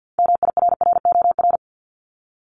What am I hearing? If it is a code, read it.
Morse code, heard as a keyed tone, about 50 wpm, 700 Hz. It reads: MIRROR
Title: Multi-letter Phonemes 50wpm